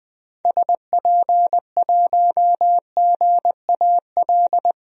Morse code, heard as a keyed tone, about 20 wpm, 700 Hz. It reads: SP1GAL
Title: DXCC-100 Call Signs - 6 Characters - 20wpm